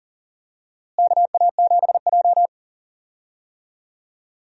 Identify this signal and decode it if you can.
Morse code, heard as a keyed tone, about 40 wpm, 700 Hz. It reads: KA7J
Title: USA Call Signs - 4 Characters - Repeated 2x - 40wpm